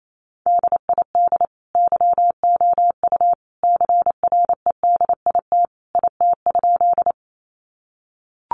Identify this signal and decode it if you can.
Morse code, heard as a keyed tone, about 28 wpm, 700 Hz. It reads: DID YOU CREDIT IT?
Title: Sentences from Top 1000 Words 28wpm